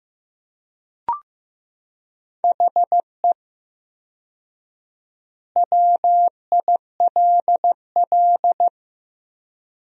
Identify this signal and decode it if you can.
Morse code, heard as a keyed tone, about 15 wpm, 700 Hz. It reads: HE WILL
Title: 4x Word-Spacing - Sentences from Top 200 Words - 15wpm